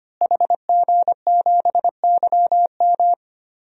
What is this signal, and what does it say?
Morse code, heard as a keyed tone, about 25 wpm, 700 Hz. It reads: HG7YM
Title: DXCC-100 Call Signs - 5 Characters - Repeated 2x - 25wpm